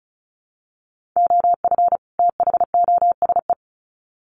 Morse code, heard as a keyed tone, about 34 wpm, 700 Hz. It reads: OF THOSE
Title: Sets of 2 Words 35wpm